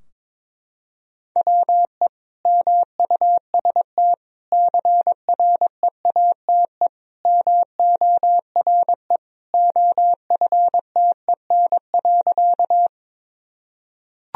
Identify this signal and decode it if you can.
Morse code, heard as a keyed tone, about 22 wpm, 700 Hz. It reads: WE MUST CREATE MORE OFTEN.
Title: Sentences from Top 1000 Words - Review Part 3 - 22wpm